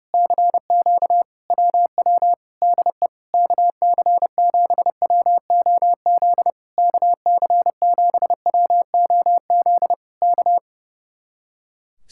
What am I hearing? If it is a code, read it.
Morse code, heard as a keyed tone, about 30 wpm, 700 Hz. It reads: CQ WW DE KC7WOZ KC7WOZ K
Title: Speed-Racing Common QSO Phrases - 30wpm